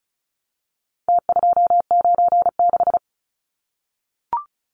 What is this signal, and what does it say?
Morse code, heard as a keyed tone, about 34 wpm, 700 Hz. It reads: T296
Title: Numbers - Contest Style - 4-Digit 35wpm